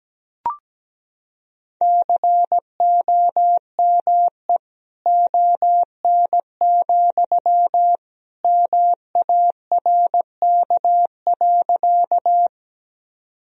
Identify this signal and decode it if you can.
Morse code, heard as a keyed tone, about 17 wpm, 700 Hz. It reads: COME ON, MARK.
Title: Sentences from Top 1000 Words - Review Part 1 - 17wpm